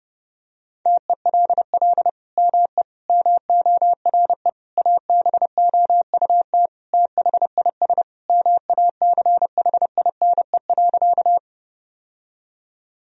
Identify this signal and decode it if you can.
Morse code, heard as a keyed tone, about 30 wpm, 700 Hz. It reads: TELL ME MORE ABOUT THIS MACHINE.
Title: Sentences from Top 1000 Words - Review Part 2 - 30wpm